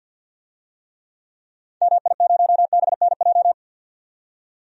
Morse code, heard as a keyed tone, about 51 wpm, 700 Hz. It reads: MI0BNJ
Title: DXCC-100 Call Signs - 6 Characters - 50wpm